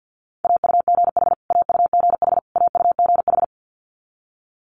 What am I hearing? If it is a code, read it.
Morse code, heard as a keyed tone, about 50 wpm, 700 Hz. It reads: AUGH AUGH AUGH
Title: Multi-letter Phonemes Repeated 3x 50wpm